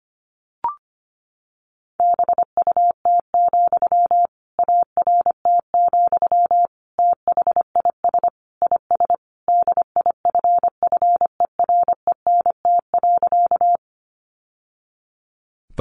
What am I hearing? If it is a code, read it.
Morse code, heard as a keyed tone, about 25 wpm, 700 Hz. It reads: BUT, ART, THIS IS DIFFERENT.
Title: Sentences from Top 600 Words 25wpm